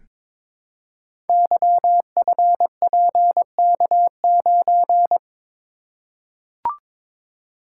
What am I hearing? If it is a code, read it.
Morse code, heard as a keyed tone, about 22 wpm, 700 Hz. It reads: YFPK9
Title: Code Groups - 5 Letter:Numbers - 22wpm